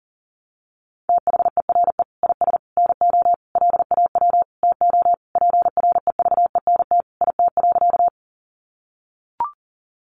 Morse code, heard as a keyed tone, about 37 wpm, 700 Hz. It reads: THERE IS NO LAW TO PREVENT IT.
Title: Sentences from Top 1100 Words 40wpm